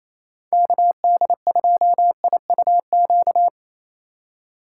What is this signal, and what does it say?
Morse code, heard as a keyed tone, about 34 wpm, 700 Hz. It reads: KD2IUQ
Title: Speed Racing - USA 2x3 Call Signs 28wpm